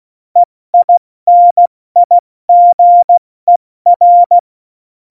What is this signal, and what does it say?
Morse code, heard as a keyed tone, about 16 wpm, spaced as for 14 wpm, 700 Hz. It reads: EINIGER
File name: Morsekurs / Witze / Witz 12. Januar 2021, 70 F80